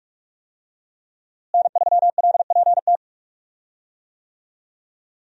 Morse code, heard as a keyed tone, about 45 wpm, 700 Hz. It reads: N3LPT
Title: USA Call Signs - 5 Characters - Repeated 2x - 45wpm